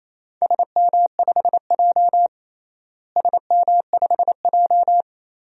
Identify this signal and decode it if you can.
Morse code, heard as a keyed tone, about 28 wpm, 700 Hz. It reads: SM5J SM5J
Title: DXCC-100 Call Signs - 4 Characters - Repeated 3x - 28wpm